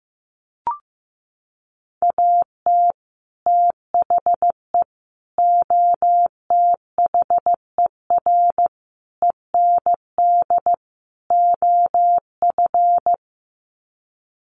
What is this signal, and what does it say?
Morse code, heard as a keyed tone, about 14 wpm, 700 Hz. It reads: AT THE OTHER END OF
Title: Sets of 5 Words 15wpm